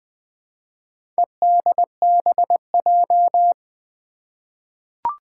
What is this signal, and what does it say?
Morse code, heard as a keyed tone, about 20 wpm, 700 Hz. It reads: EDBJ
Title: Four Challenging Letter-Number Combinations 20wpm